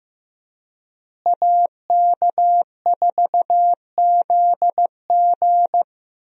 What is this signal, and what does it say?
Morse code, heard as a keyed tone, about 15 wpm, 700 Hz. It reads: AK4ZG
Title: USA Call Signs - 5 Characters - Repeated 2x - 15wpm